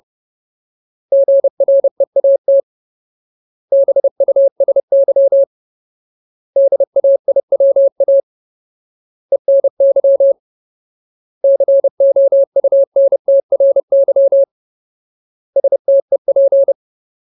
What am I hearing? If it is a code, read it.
Morse code, heard as a keyed tone, about 30 wpm, 550 Hz. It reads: GREAT BUSY DAIWA ENY COUNTRY STEP
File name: Ultimate Word List / 30WPM Ultimate